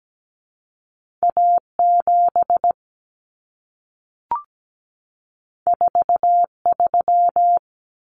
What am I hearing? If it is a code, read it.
Morse code, heard as a keyed tone, about 17 wpm, 700 Hz. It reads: A7 43
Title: Common Cut Numbers - 2-Digit 17wpm